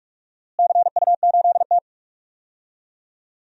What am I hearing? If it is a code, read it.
Morse code, heard as a keyed tone, about 42 wpm, 700 Hz. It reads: CU8T
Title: DXCC-100 Call Signs - 4 Characters - Repeated 2x - 45wpm